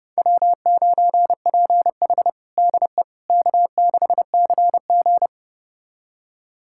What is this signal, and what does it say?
Morse code, heard as a keyed tone, about 30 wpm, 700 Hz. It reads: W9PH DE K6CG
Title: USA Callsign DE Callsign - 4 Characters - 30wpm